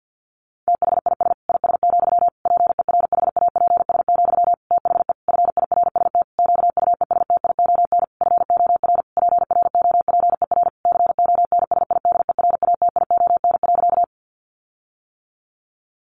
Morse code, heard as a keyed tone, about 50 wpm, 700 Hz. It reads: THIS IS, PERHAPS, THE FIRST QUESTION FOR PROPER CONSIDERATION.
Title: Sentences from Top 1500 Words 50wpm